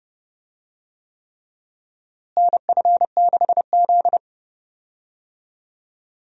Morse code, heard as a keyed tone, about 30 wpm, 700 Hz. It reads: NF6Z
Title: USA Call Signs - 4 Characters - Repeated 3x - 30wpm